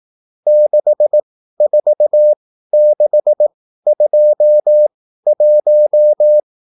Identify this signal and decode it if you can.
Morse code, heard as a keyed tone, about 18 wpm, spaced as for 13 wpm, 600 Hz. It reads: 64621